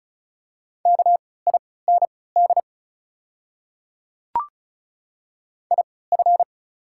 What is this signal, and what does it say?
Morse code, heard as a keyed tone, about 34 wpm, spaced as for 20 wpm, 690 Hz. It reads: KIND IF
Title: 3x Character Spacing - Top 300 Words - 35wpm